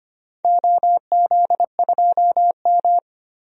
Morse code, heard as a keyed tone, about 25 wpm, 700 Hz. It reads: OZ2M
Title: DXCC-100 Call Signs - 4 Characters - 25wpm